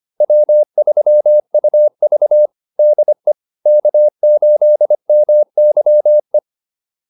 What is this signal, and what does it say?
Morse code, heard as a keyed tone, about 25 wpm, 600 Hz. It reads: W3UV DE K8MYE